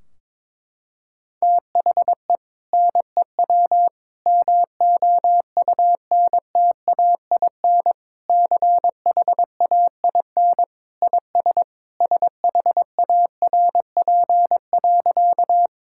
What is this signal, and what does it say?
Morse code, heard as a keyed tone, about 22 wpm, 700 Hz. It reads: THE NEW MOUNTAIN CHAIN IS SHARP.
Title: Sentences from Top 1600 Words 22wpm